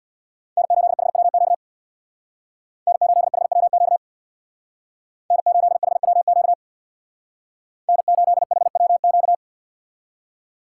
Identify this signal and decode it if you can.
Morse code, heard as a keyed tone, about 61 wpm, 700 Hz. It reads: N8HWQ N8HWQ N8HWQ N8HWQ
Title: Speed Racing - USA 1x3 Call Signs 50wpm